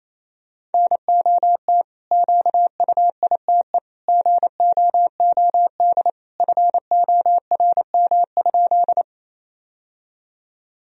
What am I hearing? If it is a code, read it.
Morse code, heard as a keyed tone, about 28 wpm, 700 Hz. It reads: NOT QUITE GOOD FORM?
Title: Sentences from Top 1000 Words - Review Part 1 - 28wpm